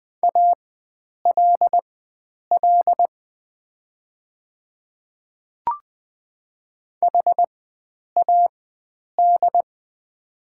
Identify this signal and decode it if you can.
Morse code, heard as a keyed tone, about 20 wpm, spaced as for 9 wpm, 690 Hz. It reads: ALL HAD
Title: 4x Character Spacing - Top 100 Words - 20wpm